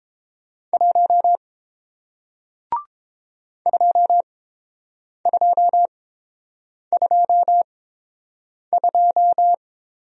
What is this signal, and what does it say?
Morse code, heard as a keyed tone, about 31 wpm, spaced as for 7 wpm, 700 Hz. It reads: 1 2222